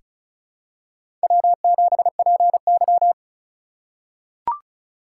Morse code, heard as a keyed tone, about 36 wpm, 700 Hz. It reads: W7PY
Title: USA Call Signs - 2x Character Spacing - 35wpm